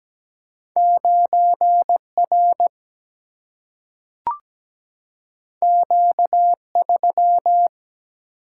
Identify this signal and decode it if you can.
Morse code, heard as a keyed tone, about 17 wpm, 700 Hz. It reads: 9R Q3
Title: Single-Letter Single-Number 17wpm